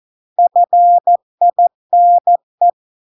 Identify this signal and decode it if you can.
Morse code, heard as a keyed tone, about 14 wpm, 700 Hz. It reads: FINE